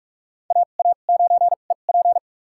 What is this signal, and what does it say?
Morse code, heard as a keyed tone, about 46 wpm, spaced as for 33 wpm, 700 Hz. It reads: AA9EP
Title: USA Call Signs - 5 Characters - 2x Character Spacing - 45wpm